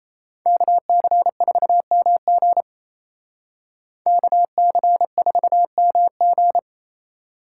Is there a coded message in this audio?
Yes: Morse code from 0.5 to 6.6 s, heard as a keyed tone, about 36 wpm, 700 Hz. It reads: KC4MG KC4MG